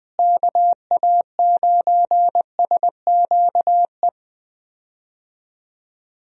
Morse code, heard as a keyed tone, about 20 wpm, 700 Hz. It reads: KA9SQE
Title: USA Call Signs - 6 Characters - Repeated 2x - 20wpm